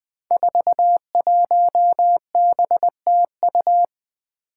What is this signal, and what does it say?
Morse code, heard as a keyed tone, about 20 wpm, 700 Hz. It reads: 41BTU
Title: Five Challenging Letter-Number Combinations 20wpm